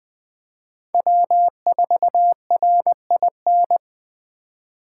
Morse code, heard as a keyed tone, about 21 wpm, 700 Hz. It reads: W4RIN
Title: Speed Racing - USA 1x3 Call Signs 17wpm